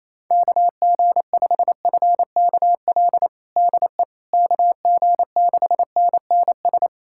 Morse code, heard as a keyed tone, about 28 wpm, 700 Hz. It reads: KG5FKL DE KG6NNS
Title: USA Callsign DE Callsign 28wpm